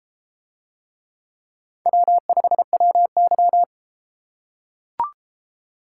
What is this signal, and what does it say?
Morse code, heard as a keyed tone, about 31 wpm, 700 Hz. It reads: W5WY